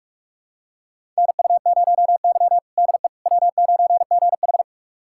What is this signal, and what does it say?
Morse code, heard as a keyed tone, about 42 wpm, 700 Hz. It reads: NU0Y DE W9GH
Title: USA Callsign DE Callsign - 4 Characters - 45wpm